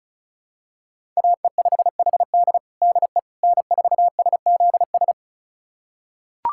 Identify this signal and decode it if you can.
Morse code, heard as a keyed tone, about 36 wpm, 700 Hz. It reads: AE5HD DE N4SZS
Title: USA Callsign DE Callsign - 5 Characters - 35wpm